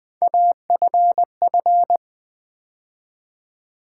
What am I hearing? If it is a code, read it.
Morse code, heard as a keyed tone, about 20 wpm, 700 Hz. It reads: AFF
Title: Build A Word - 3 Letters At A Time - 9-Letter Words - 20wpm